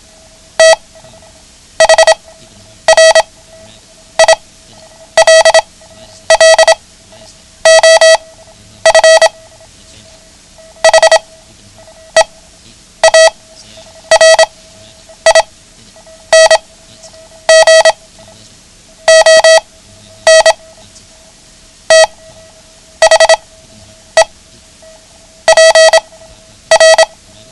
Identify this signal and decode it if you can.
Morse code, heard as a keyed tone, about 26 wpm, spaced as for 9 wpm, 650 Hz. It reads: THRILL OF HEARING ON THE PR
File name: ham radio / CodeQuick / Code Quick 09